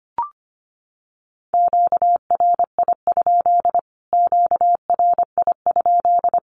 Morse code, heard as a keyed tone, about 25 wpm, 700 Hz. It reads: QRI? QRI?
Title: Q-Codes Repeated 3x 25wpm